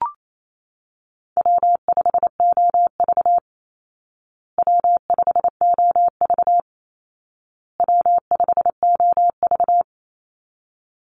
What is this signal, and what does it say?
Morse code, heard as a keyed tone, about 28 wpm, 700 Hz. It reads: W5OV W5OV W5OV
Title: Easy Call Signs Repeated 3x - 28wpm